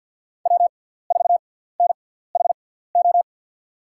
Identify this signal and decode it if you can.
Morse code, heard as a keyed tone, about 53 wpm, spaced as for 17 wpm, 700 Hz. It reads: W4NHO
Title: USA Call Signs - 5 Characters - 6x Character Spacing - 50wpm